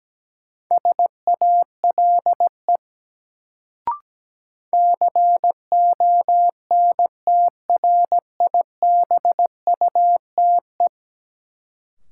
Morse code, heard as a keyed tone, about 17 wpm, 700 Hz. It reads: SALE CONTRIBUTE
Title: Sentences from Top 1400 Words - New Word Drill Alt - 17wpm